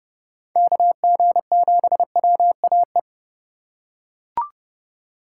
Morse code, heard as a keyed tone, about 30 wpm, 700 Hz. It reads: KG7WAE